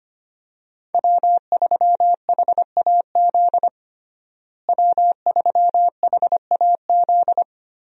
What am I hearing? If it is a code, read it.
Morse code, heard as a keyed tone, about 25 wpm, 700 Hz. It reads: W3HAZ W3HAZ
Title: USA Call Signs - 5 Characters - Repeated 3x - 25wpm